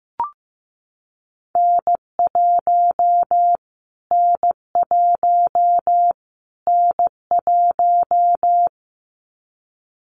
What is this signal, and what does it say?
Morse code, heard as a keyed tone, about 15 wpm, 700 Hz. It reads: N1 N1 N1
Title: Callsign Prefixes for Top 25 DXCC - Repeated 3x - 15wpm